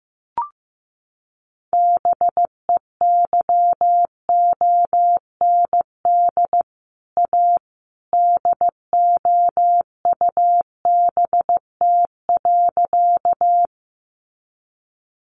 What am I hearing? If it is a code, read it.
Morse code, heard as a keyed tone, about 15 wpm, 700 Hz. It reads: BEYOND A DOUBT.
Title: Sentences from Top 1500 Words 15wpm